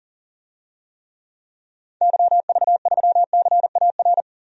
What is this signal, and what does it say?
Morse code, heard as a keyed tone, about 40 wpm, 700 Hz. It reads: YV3CAR
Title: DXCC-100 Call Signs - 6 Characters - Repeated 2x - 40wpm